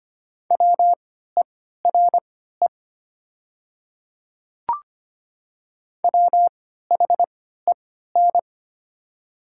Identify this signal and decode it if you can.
Morse code, heard as a keyed tone, about 25 wpm, spaced as for 14 wpm, 700 Hz. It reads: WERE WHEN
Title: 3x Character Spacing - Top 100 Words - 25wpm